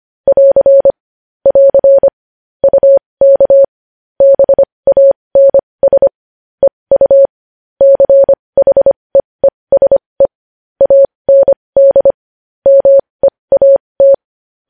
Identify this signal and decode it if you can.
Morse code, heard as a keyed tone, about 25 wpm, spaced as for 20 wpm, 550 Hz. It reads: <AR> <AR> UK BANS EU CHEESE AND MEAT
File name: podcast / News headlines-1x-25x20